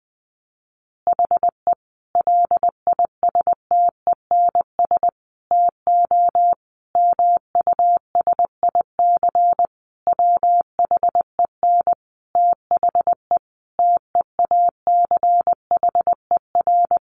Morse code, heard as a keyed tone, about 20 wpm, 700 Hz. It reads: HE LISTENS TO MUSIC WHEN THE TEACHER
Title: Additional Easy Phrases 20wpm